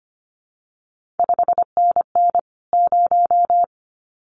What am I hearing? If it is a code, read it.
Morse code, heard as a keyed tone, about 25 wpm, 700 Hz. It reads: 5NN 0
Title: CQ WW DX Contest Exchange Encore - 25wpm